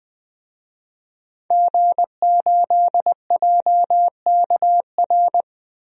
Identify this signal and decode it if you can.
Morse code, heard as a keyed tone, about 20 wpm, 700 Hz. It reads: G8JKR
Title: DXCC-100 Call Signs - 5 Characters - 20wpm